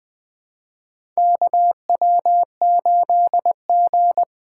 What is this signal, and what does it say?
Morse code, heard as a keyed tone, about 20 wpm, 700 Hz. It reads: KW8G
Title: USA Call Signs - 4 Characters - 4x Character Spacing - 20wpm